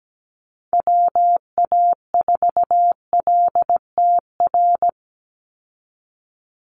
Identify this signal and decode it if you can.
Morse code, heard as a keyed tone, about 17 wpm, 700 Hz. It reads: WA4LTR
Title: USA Call Signs 17wpm